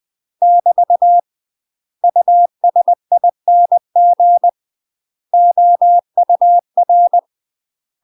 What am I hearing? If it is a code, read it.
Morse code, heard as a keyed tone, about 20 wpm, 700 Hz. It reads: <BT> USING OUR
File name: UK MASTER QSOs TRAINING/20WPM